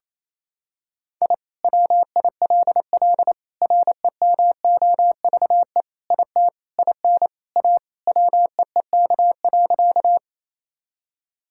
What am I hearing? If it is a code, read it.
Morse code, heard as a keyed tone, about 27 wpm, 700 Hz. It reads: I WILL REMOVE IT IN A WEEK.
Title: Sentences from Top 1000 Words - Review Part 5 - 28wpm